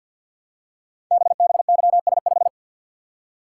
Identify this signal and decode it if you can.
Morse code, heard as a keyed tone, about 50 wpm, 700 Hz. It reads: BBYS5
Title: Code Groups - 5 Letter:Numbers - 50wpm